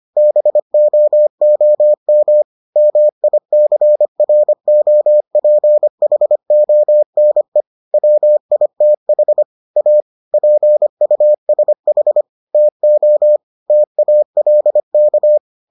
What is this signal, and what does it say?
Morse code, heard as a keyed tone, about 25 wpm, 600 Hz. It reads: BOOM MICROPHONE WITH A PUSH TO TALK